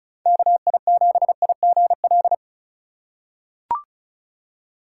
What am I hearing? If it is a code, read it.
Morse code, heard as a keyed tone, about 36 wpm, 690 Hz. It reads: KI7IGL
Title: USA Call Signs - 3x Character Spacing - 35wpm